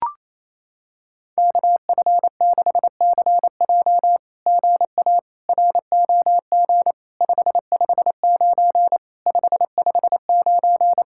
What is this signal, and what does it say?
Morse code, heard as a keyed tone, about 28 wpm, 700 Hz. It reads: KF6CJ GA ROG 559 559
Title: SOTA QSO Practice 28wpm